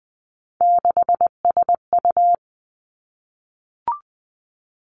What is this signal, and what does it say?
Morse code, heard as a keyed tone, about 20 wpm, 700 Hz. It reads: 6SU
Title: Three Challenging Letter-Number Combinations 20wpm